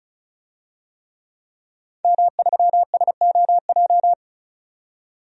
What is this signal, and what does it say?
Morse code, heard as a keyed tone, about 36 wpm, 700 Hz. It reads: M3SOJ
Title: DXCC-100 Call Signs - 5 Characters - 35wpm